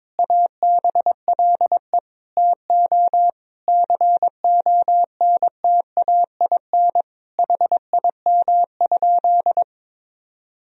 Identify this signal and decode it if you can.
Morse code, heard as a keyed tone, about 22 wpm, 700 Hz. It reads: ABLE TO CONTAIN HIM?
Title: Sentences from Top 1000 Words - Review Part 2 - 22wpm